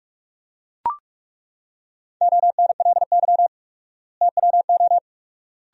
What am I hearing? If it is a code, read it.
Morse code, heard as a keyed tone, about 42 wpm, 700 Hz. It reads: ONLY TWO
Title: 4x Word-Spacing - Sentences from Top 200 Words - 45wpm